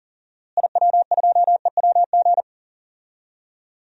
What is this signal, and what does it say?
Morse code, heard as a keyed tone, about 40 wpm, 700 Hz. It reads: IW2EWG
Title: DXCC-100 Call Signs - 6 Characters - Repeated 3x - 40wpm